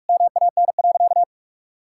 Morse code, heard as a keyed tone, about 46 wpm, 700 Hz. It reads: MAN.
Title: Sentences from Top 1000 Words - Review Part 5 - 45wpm